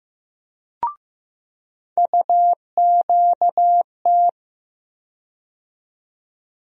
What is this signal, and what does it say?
Morse code, heard as a keyed tone, about 15 wpm, 700 Hz. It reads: UQT
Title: USA Call Sign Postfixes - 3 Letters 15wpm